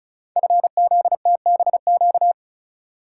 Morse code, heard as a keyed tone, about 36 wpm, 700 Hz. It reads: FZTBQ
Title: Code Groups - 5 Letters - 35wpm